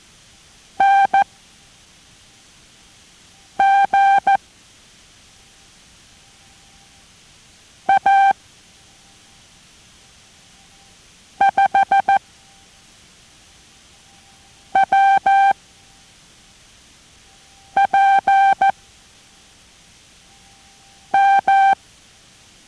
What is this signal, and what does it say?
Morse code, heard as a keyed tone, about 14 wpm, 790 Hz. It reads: N G A 5 W P M